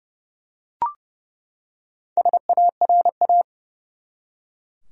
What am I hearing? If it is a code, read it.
Morse code, heard as a keyed tone, about 32 wpm, 700 Hz. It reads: SARA
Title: USA Common Names - 30wpm